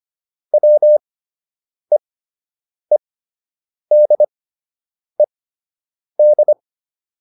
Morse code, heard as a keyed tone, about 25 wpm, spaced as for 8 wpm, 600 Hz. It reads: WEEDED
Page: fundamental-practice-files